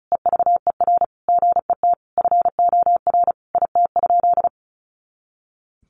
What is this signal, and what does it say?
Morse code, heard as a keyed tone, about 34 wpm, 700 Hz. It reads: EVER GET FOR IT?
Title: Sentences from Top 400 Words 35wpm